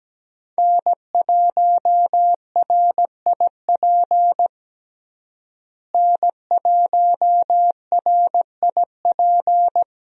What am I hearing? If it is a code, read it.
Morse code, heard as a keyed tone, about 17 wpm, 700 Hz. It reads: N1RIP N1RIP
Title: USA Call Signs - 5 Characters - Repeated 3x - 17wpm